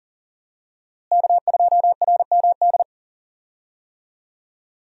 Morse code, heard as a keyed tone, about 40 wpm, 700 Hz. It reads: K2RMD
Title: USA Call Signs - 5 Characters - Repeated 2x - 40wpm